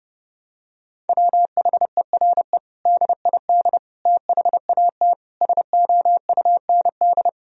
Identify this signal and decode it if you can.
Morse code, heard as a keyed tone, about 30 wpm, 700 Hz. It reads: WHERE DID THAT SOUND
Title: Sentences from Top 1000 Words - Review Part 1 - 30wpm